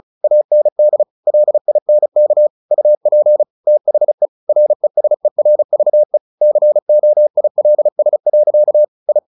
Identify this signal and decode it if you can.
Morse code, heard as a keyed tone, about 34 wpm, 600 Hz. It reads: AND LINK UP THE RESERVE COILS. I
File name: the old man and the sea / the old man and the sea 1 35wpm